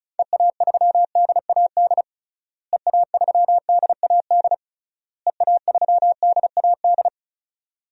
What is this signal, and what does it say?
Morse code, heard as a keyed tone, about 36 wpm, 700 Hz. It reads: EA3DAD EA3DAD EA3DAD
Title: DXCC-100 Call Signs - 6 Characters - Repeated 3x - 35wpm